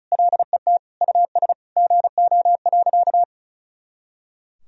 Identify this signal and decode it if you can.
Morse code, heard as a keyed tone, about 36 wpm, 700 Hz. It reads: LET US GO.
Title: Sentences from Top 400 Words and Abbreviations 35wpm